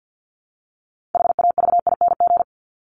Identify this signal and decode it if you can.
Morse code, heard as a keyed tone, about 49 wpm, 700 Hz. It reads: HAVING
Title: ING Suffix - Top 100 Words with Base Words 50wpm